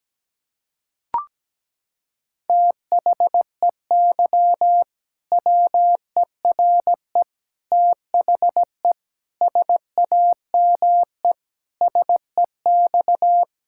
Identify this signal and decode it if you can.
Morse code, heard as a keyed tone, about 17 wpm, 700 Hz. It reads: THEY WERE THE SAME SEX
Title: Sentences from Top 1600 Words 17wpm